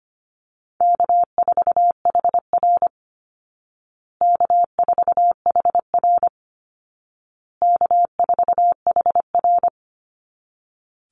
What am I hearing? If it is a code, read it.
Morse code, heard as a keyed tone, about 25 wpm, 700 Hz. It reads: K4HR K4HR K4HR
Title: Call Signs Repeated 2x - 25wpm